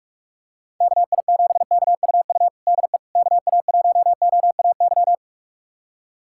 Morse code, heard as a keyed tone, about 41 wpm, 700 Hz. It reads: KI7KUU DE KA1OAY
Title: USA Callsign DE Callsign 45wpm